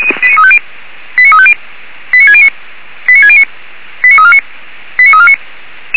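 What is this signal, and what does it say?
Five-tone selective call (ZVEI), heard as quick runs of five tones, tones of about 70 ms. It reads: 89350 89609 89609 89349 89349
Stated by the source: ZVEI five-tone call received by SDR, archived by feuerwehr-orlamuende.org.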